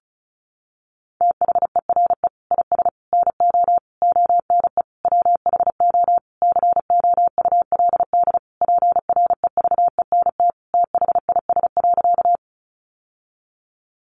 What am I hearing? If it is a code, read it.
Morse code, heard as a keyed tone, about 34 wpm, 700 Hz. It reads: THERE IS NO ONE WHO COULD PREVENT THIS.
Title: Sentences from Top 1100 Words 35wpm